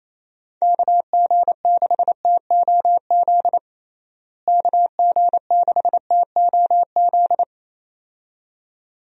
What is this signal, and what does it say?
Morse code, heard as a keyed tone, about 28 wpm, 700 Hz. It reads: KG6TOZ KG6TOZ
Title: USA Call Signs - 6 Characters - Repeated 2x - 28wpm